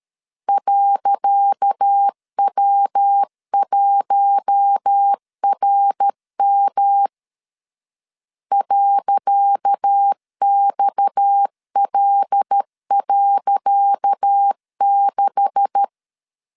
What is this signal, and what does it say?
Morse code, heard as a keyed tone, about 13 wpm, 800 Hz. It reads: .W1RM .XL.6
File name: LEZIONI / 10 SIGNS / 10C DOT 12wpm 2451